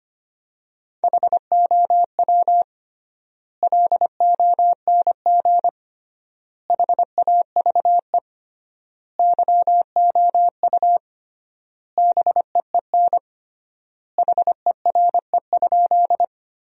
Morse code, heard as a keyed tone, about 25 wpm, 700 Hz. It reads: HOW LONG HAVE YOU BEEN HERE?
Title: 3x Word-Spacing - Sentences from Top 200 Words - 25wpm